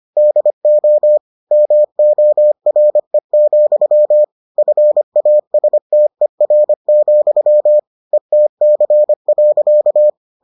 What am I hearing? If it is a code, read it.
Morse code, heard as a keyed tone, about 25 wpm, 600 Hz. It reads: DO MORE, FASTER, ETC.